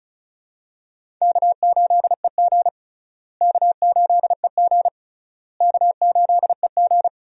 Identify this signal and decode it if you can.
Morse code, heard as a keyed tone, about 36 wpm, 700 Hz. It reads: K8EG K8EG K8EG
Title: USA Call Signs - 4 Characters - Repeated 3x - 35wpm